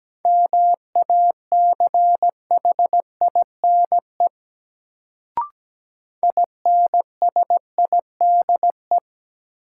Morse code, heard as a keyed tone, about 17 wpm, 700 Hz. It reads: MACHINE INSIDE
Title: Sentences from Top 1000 Words - Review Part 2 - New Word Drill - 17wpm